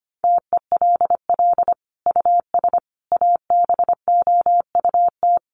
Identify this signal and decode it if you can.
Morse code, heard as a keyed tone, about 25 wpm, 700 Hz. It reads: TELL US ABOUT
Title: Sentences from Top 1400 Words 25wpm